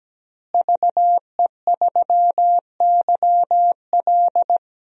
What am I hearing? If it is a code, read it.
Morse code, heard as a keyed tone, about 17 wpm, 700 Hz. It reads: VE3YL
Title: DXCC-100 Call Signs - 5 Characters - Repeated 3x - 17wpm